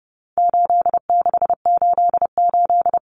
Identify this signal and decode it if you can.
Morse code, heard as a keyed tone, about 30 wpm, 700 Hz. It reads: 8688
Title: Numbers - 4-Digit 30wpm